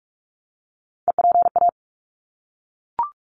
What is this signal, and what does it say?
Morse code, heard as a keyed tone, about 44 wpm, 700 Hz. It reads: EPA
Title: ARRL Sections 45wpm